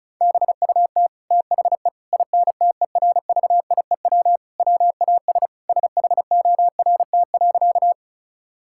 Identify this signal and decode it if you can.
Morse code, heard as a keyed tone, about 36 wpm, 700 Hz. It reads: BUT THE INTERVIEW WAS SHORT.
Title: Sentences from Top 1600 Words 35wpm